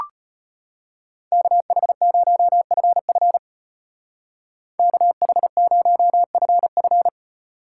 Morse code, heard as a keyed tone, about 36 wpm, 700 Hz. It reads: KH0FF KH0FF